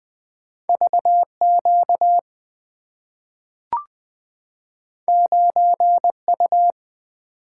Morse code, heard as a keyed tone, about 20 wpm, 700 Hz. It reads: V Q 9 U
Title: Two Letter-Number Combinations - 20wpm